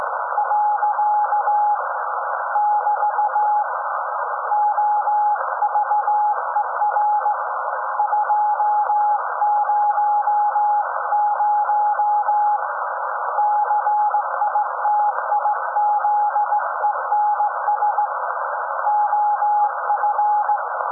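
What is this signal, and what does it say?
Morse code, heard as a keyed tone, about 17 wpm, 840 Hz, with noise. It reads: . <BT> OVER 200 CITIZENS OF
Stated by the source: SAQ Grimeton Radio, Alexanderson 200 kW alternator on 17.2 kHz, received in Potsdam with a very strong signal on World Radio Day.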